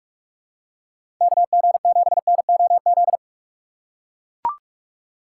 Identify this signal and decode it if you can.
Morse code, heard as a keyed tone, about 41 wpm, 700 Hz. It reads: KG7NOZ
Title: USA Call Signs - 6 Characters - 2x Character Spacing - 45wpm